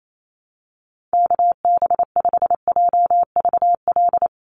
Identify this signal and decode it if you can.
Morse code, heard as a keyed tone, about 28 wpm, 700 Hz. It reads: KB5JVL
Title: USA Call Signs 28wpm